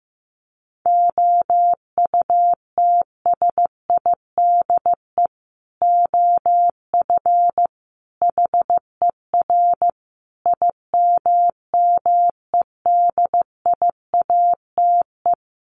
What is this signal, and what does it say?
Morse code, heard as a keyed tone, about 15 wpm, 700 Hz. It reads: OUTSIDE OF HER IMMEDIATE
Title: Sentences from Top 1200 Words 15wpm